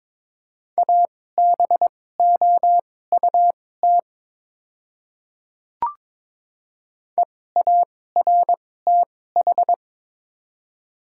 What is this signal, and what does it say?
Morse code, heard as a keyed tone, about 23 wpm, spaced as for 16 wpm, 700 Hz. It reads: ABOUT EARTH
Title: 2x Character Spacing - Top 200 Words - 22wpm